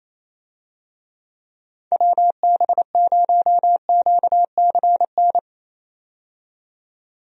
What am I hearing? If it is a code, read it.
Morse code, heard as a keyed tone, about 27 wpm, 700 Hz. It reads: WB0QCN
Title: USA Call Signs - 6 Characters - 28wpm